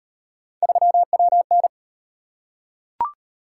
Morse code, heard as a keyed tone, about 38 wpm, 700 Hz. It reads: 3WN